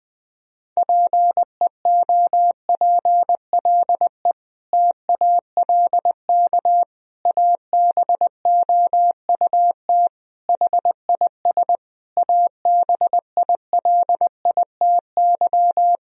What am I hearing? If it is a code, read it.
Morse code, heard as a keyed tone, about 21 wpm, 700 Hz. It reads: PEOPLE TALK ABOUT HIS ABILITY
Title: Sentences from Top 1000 Words - Review Part 3 - 20wpm